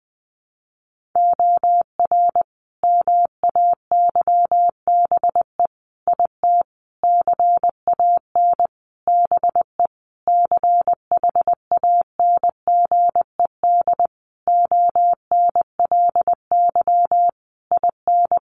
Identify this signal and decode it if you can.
Morse code, heard as a keyed tone, about 20 wpm, 700 Hz. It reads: OR MAYBE IT CAN BE CHANGED ONLY IN
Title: A Sound Of Thunder 20wpm